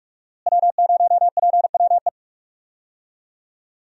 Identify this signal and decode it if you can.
Morse code, heard as a keyed tone, about 44 wpm, 700 Hz. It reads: W0PWE
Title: USA Call Signs - 5 Characters - Repeated 3x - 45wpm